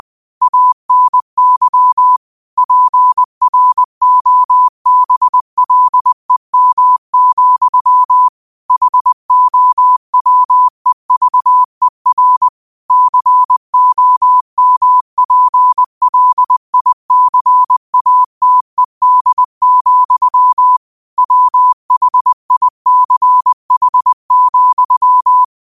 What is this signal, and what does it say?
Morse code, heard as a keyed tone, about 20 wpm, 1000 Hz. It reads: ANY PROBLEM, HOWEVER COMPLICATED, WHICH,
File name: qotdfiles / 20wpm